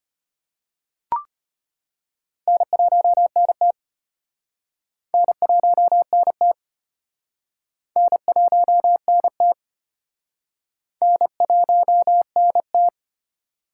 Morse code, heard as a keyed tone, about 35 wpm, 700 Hz. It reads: N1NT N1NT N1NT N1NT